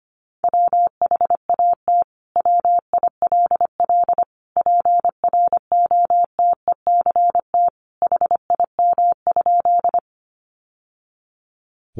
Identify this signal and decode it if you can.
Morse code, heard as a keyed tone, about 25 wpm, 700 Hz. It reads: WHAT WILL PROTECT HIM?
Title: Sentences from Top 900 Words 25wpm